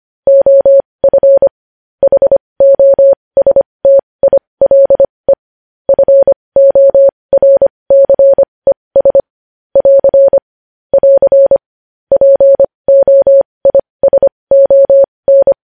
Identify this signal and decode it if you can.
Morse code, heard as a keyed tone, about 25 wpm, spaced as for 20 wpm, 550 Hz. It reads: OF HOSTILE FORCES <AR> <AR> POISON
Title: News Headlines at 20